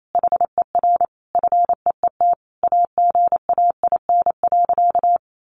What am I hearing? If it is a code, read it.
Morse code, heard as a keyed tone, about 28 wpm, 700 Hz. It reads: HER FEET AGAIN.
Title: Sentences from Top 1300 Words 28wpm